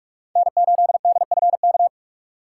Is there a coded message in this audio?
Yes: Morse code from 0.4 to 1.9 s, heard as a keyed tone, about 44 wpm, 700 Hz. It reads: N8DFK